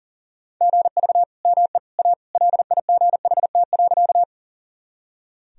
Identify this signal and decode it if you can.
Morse code, heard as a keyed tone, about 40 wpm, 700 Hz. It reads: GV ME A LIGHT.
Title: Sentences from Top 200 Words and Abbreviations 40wpm